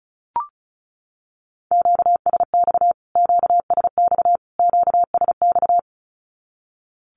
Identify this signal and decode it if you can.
Morse code, heard as a keyed tone, about 34 wpm, 700 Hz. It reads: QSX QSX QSX
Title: Q-Codes Repeated 3x 35wpm